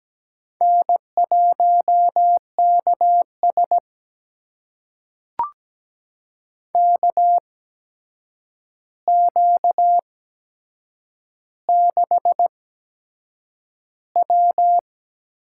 Morse code, heard as a keyed tone, about 17 wpm, spaced as for 7 wpm, 700 Hz. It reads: N1KS KQ6W
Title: USA Call Signs - 4 Characters - 8x Character Spacing - 17wpm